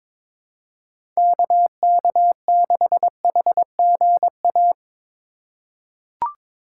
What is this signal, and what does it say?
Morse code, heard as a keyed tone, about 22 wpm, 700 Hz. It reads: KK6HGA
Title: USA Call Signs - 6 Characters - 4x Character Spacing - 22wpm